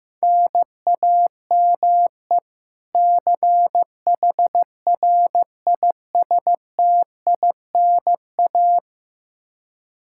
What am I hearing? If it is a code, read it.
Morse code, heard as a keyed tone, about 15 wpm, 700 Hz. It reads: NAME CHRISTINA
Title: Practice QSO - Exchange 1 - Beginning - 15wpm